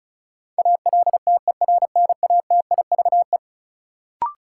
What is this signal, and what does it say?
Morse code, heard as a keyed tone, about 36 wpm, 700 Hz. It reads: ALTERNATIVE
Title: Sentences from Top 1000 Words - Review Part 4 - New Word Drill - 35wpm